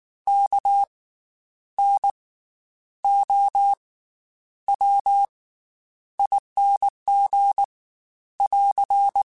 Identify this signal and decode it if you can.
Morse code, heard as a keyed tone, about 19 wpm, spaced as for 9 wpm, 800 Hz. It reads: KNOWING <AR>
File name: Special INGs / Special INGs-mp3s / Slow ing @18